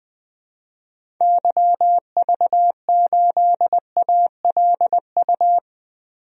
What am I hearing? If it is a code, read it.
Morse code, heard as a keyed tone, about 20 wpm, 700 Hz. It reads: YV8ALU
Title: DXCC-100 Call Signs - 6 Characters - 20wpm